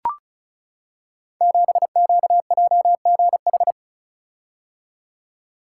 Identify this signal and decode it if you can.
Morse code, heard as a keyed tone, about 36 wpm, 690 Hz. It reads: 7QJGH
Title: Code Groups - 5 Letter:Numbers - 35wpm